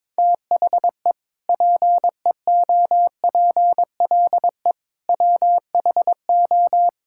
Morse code, heard as a keyed tone, about 22 wpm, 700 Hz. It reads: THE PEOPLE WHO
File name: practice / 22wpm / Sentences with 3-Word Phrases - 22wpm